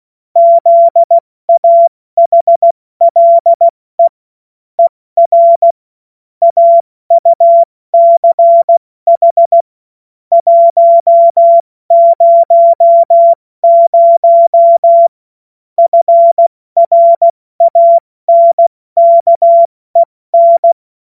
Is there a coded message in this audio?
Yes: Morse code from 0.4 to 20.7 s, heard as a keyed tone, about 16 wpm, 680 Hz. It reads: ZAHLE ER AUCH 100 FRANKEN